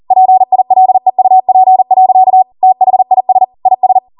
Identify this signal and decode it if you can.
Morse code, heard as a keyed tone, about 40 wpm, 750 Hz. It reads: PILEUP. THIS IS